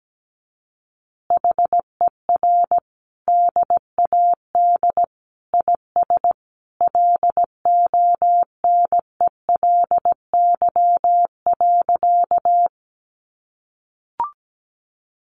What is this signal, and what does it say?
Morse code, heard as a keyed tone, about 17 wpm, 700 Hz. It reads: HER DAD IS LONELY.
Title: Additional Easy Phrases 17wpm